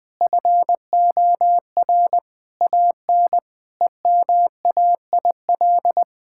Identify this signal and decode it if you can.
Morse code, heard as a keyed tone, about 20 wpm, 700 Hz. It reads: FOR AN EMAIL
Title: Sentences with 7-Word Phrases - 20wpm